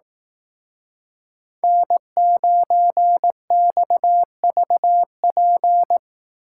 Morse code, heard as a keyed tone, about 18 wpm, 700 Hz. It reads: N9XVP
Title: Speed Racing - USA Call Signs 15wpm